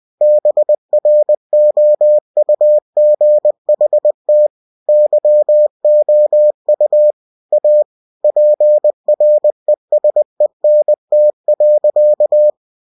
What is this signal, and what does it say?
Morse code, heard as a keyed tone, about 20 wpm, 600 Hz. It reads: BROUGHT YOU A PRESENT.
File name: cw-books / winnie-the-pooh / winnie-20wpm-page-0054